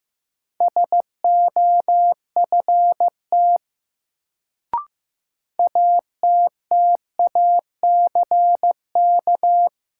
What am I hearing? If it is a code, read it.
Morse code, heard as a keyed tone, about 15 wpm, 700 Hz. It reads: SOFT ATTACK
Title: Sentences from Top 1500 Words - New Word Drill Alt - 15wpm